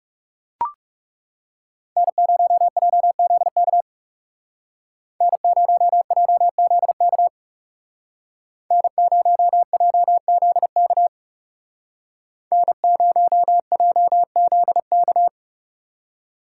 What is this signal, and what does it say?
Morse code, heard as a keyed tone, about 38 wpm, 700 Hz. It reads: N0JZK N0JZK N0JZK N0JZK